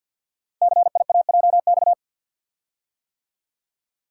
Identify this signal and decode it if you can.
Morse code, heard as a keyed tone, about 48 wpm, 700 Hz. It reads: CIAJX
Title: Code Groups - 5 Letters - 50wpm